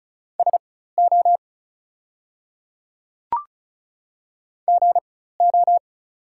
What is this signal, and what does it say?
Morse code, heard as a keyed tone, about 34 wpm, spaced as for 16 wpm, 700 Hz. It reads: SO GO
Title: 4x Character Spacing - Top 100 Words - 35wpm